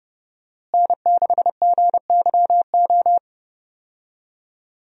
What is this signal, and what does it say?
Morse code, heard as a keyed tone, about 30 wpm, 700 Hz. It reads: N6GYO
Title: USA Call Signs - 5 Characters - Repeated 3x - 30wpm